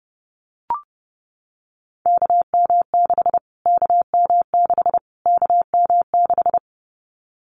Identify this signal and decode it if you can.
Morse code, heard as a keyed tone, about 30 wpm, 690 Hz. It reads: KM6 KM6 KM6
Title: Callsign Prefixes for Top 25 DXCC - Repeated 3x - 30wpm